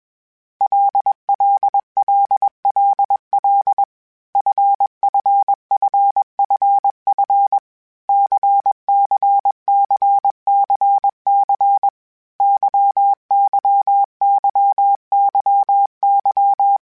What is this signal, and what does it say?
Morse code, heard as a keyed tone, about 21 wpm, 800 Hz. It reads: LLLLL FFFFF CCCCC YYYYY
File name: Basic Sets / Basic Letter Groups Collective7 / LFCYQ@20